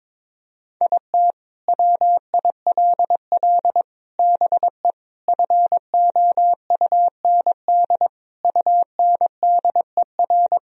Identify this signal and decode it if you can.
Morse code, heard as a keyed tone, about 22 wpm, 700 Hz. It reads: IT WILL BE FOUND UNDER
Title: Sentences from Top 1000 Words - Review Part 1 - 22wpm